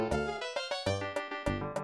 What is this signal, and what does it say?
There is no code.